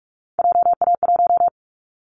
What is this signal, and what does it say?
Morse code, heard as a keyed tone, about 46 wpm, 700 Hz. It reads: JA1
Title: Callsign Prefixes for Top 50 DXCC - 45wpm